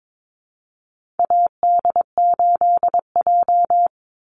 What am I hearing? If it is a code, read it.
Morse code, heard as a keyed tone, about 22 wpm, 700 Hz. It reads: AD8J
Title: Call Signs - 22wpm